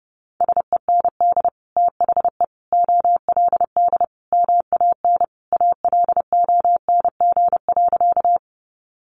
Morse code, heard as a keyed tone, about 30 wpm, 700 Hz. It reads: SEND THE OLD MAN ALONG.
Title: Sentences from Top 1500 Words 30wpm